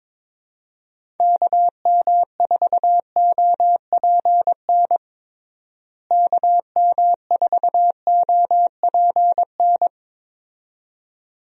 Morse code, heard as a keyed tone, about 22 wpm, 700 Hz. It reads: KM4OPN KM4OPN
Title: USA Call Signs - 6 Characters - Repeated 2x - 22wpm